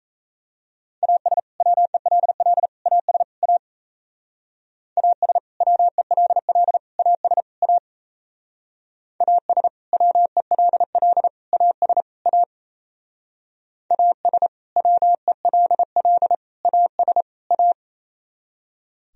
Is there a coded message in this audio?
Yes: Morse code from 1.0 to 17.7 s, heard as a keyed tone, about 37 wpm, 700 Hz. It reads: AS WELL AS A AS WELL AS A AS WELL AS A AS WELL AS A